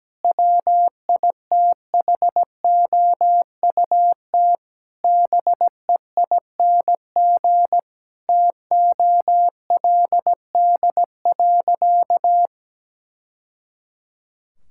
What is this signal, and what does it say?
Morse code, heard as a keyed tone, about 17 wpm, 700 Hz. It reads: WITHOUT BEING TOLD.
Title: Sentences from Top 1000 Words - Review Part 6 - 17wpm